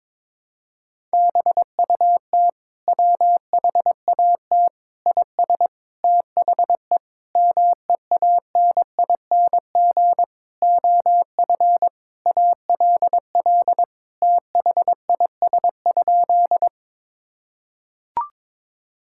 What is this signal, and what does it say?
Morse code, heard as a keyed tone, about 21 wpm, 700 Hz. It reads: BUT WHAT IS THE MEANING OF ALL THIS?
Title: Sentences from Top 1000 Words - Review Part 6 - 22wpm